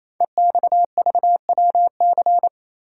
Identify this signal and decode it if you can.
Morse code, heard as a keyed tone, about 27 wpm, 700 Hz. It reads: EXVWC
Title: Code Groups - 5 Letters - 28wpm